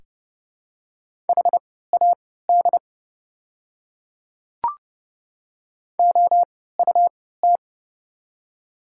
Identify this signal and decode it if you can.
Morse code, heard as a keyed tone, about 30 wpm, spaced as for 17 wpm, 700 Hz. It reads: HAD OUT
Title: 3x Character Spacing - Top 100 Words - 30wpm